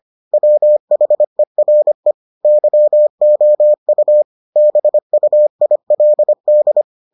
Morse code, heard as a keyed tone, about 25 wpm, 600 Hz. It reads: WHERE YOU BUILD